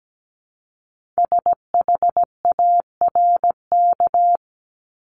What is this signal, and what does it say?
Morse code, heard as a keyed tone, about 17 wpm, 700 Hz. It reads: SHARK
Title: Multi-letter Phonemes 17wpm